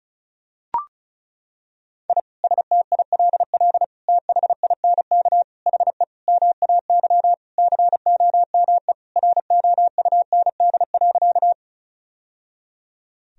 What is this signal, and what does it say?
Morse code, heard as a keyed tone, about 34 wpm, 700 Hz. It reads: I STILL THINK HE MAY COME ROUND.
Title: Sentences from Top 1000 Words - Review Part 1 - 35wpm